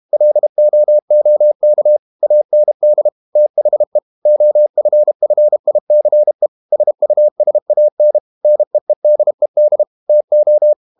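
Morse code, heard as a keyed tone, about 32 wpm, 600 Hz. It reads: LOOK AND THE OFFICE SUSAN NEEDED TO
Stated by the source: Morse code practice at 32 wpm from CW med Gnister, reading Ken Follett's novel Never in English.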